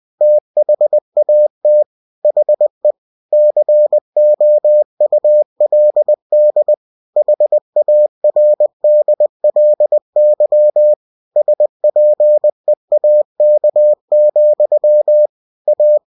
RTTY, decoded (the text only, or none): none